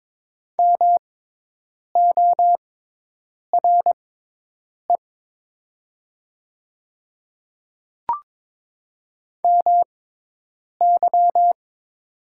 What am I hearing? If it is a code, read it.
Morse code, heard as a keyed tone, about 22 wpm, spaced as for 8 wpm, 700 Hz. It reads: MORE MY
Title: 6x Character Spacing - Top 100 Words - 22wpm